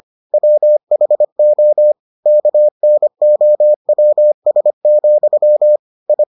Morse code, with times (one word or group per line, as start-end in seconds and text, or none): 0.3-1.9 WHO
2.3-5.8 KNOWS,
6.1-6.2 I